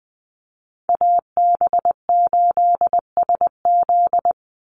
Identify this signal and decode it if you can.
Morse code, heard as a keyed tone, about 20 wpm, 700 Hz. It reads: AB8SZ
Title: USA Call Signs 20wpm